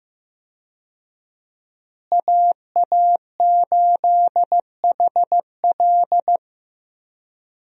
Morse code, heard as a keyed tone, about 15 wpm, 700 Hz. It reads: AA8HL
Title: USA Call Signs - 5 Characters - Repeated 3x - 15wpm